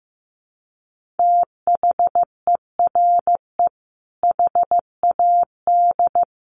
Morse code, heard as a keyed tone, about 15 wpm, 700 Hz. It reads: THERE HAD
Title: Sentences from Top 1400 Words 15wpm